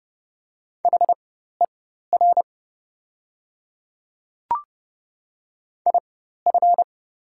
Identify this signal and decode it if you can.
Morse code, heard as a keyed tone, about 30 wpm, spaced as for 14 wpm, 700 Hz. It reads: HER IF
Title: 4x Character Spacing - Top 200 Words - 30wpm